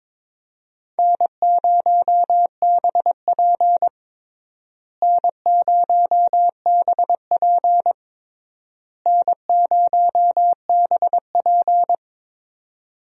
Morse code, heard as a keyed tone, about 21 wpm, 700 Hz. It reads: N0BP N0BP N0BP
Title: USA Call Signs - 4 Characters - Repeated 3x - 22wpm